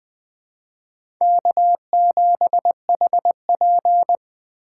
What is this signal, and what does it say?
Morse code, heard as a keyed tone, about 20 wpm, 700 Hz. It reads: K7HP
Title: USA Call Signs - 4 Characters - 4x Character Spacing - 20wpm